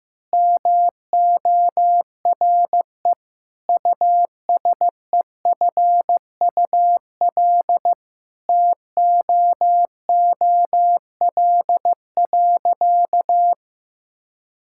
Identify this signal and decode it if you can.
Morse code, heard as a keyed tone, about 15 wpm, 700 Hz. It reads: MORE USEFUL TOOL.
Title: Sentences from Top 1000 Words - Review Part 6 - 15wpm